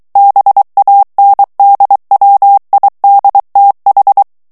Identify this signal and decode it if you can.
Morse code, heard as a keyed tone, about 23 wpm, 800 Hz. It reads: BANDWIDTH
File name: ANGLAIS texte clair 3 vitesse 22 mots minute